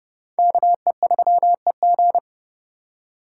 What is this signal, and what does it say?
Morse code, heard as a keyed tone, about 29 wpm, 700 Hz. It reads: KE3EG